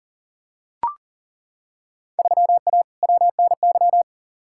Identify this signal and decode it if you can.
Morse code, heard as a keyed tone, about 40 wpm, 700 Hz. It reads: 3A WNY
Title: Field Day Exchanges - 40wpm